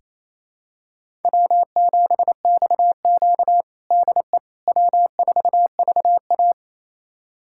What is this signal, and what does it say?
Morse code, heard as a keyed tone, about 28 wpm, 700 Hz. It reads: W7XQ DE W4VA
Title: USA Callsign DE Callsign - 4 Characters - 28wpm